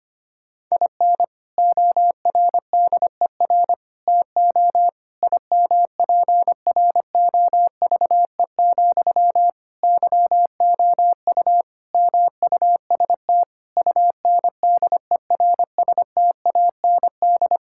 Morse code, heard as a keyed tone, about 25 wpm, 700 Hz. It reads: IN ORDER TO IMPROVE, YOU MUST UNDERSTAND
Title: Sentences from Top 1000 Words - Review Part 5 - 25wpm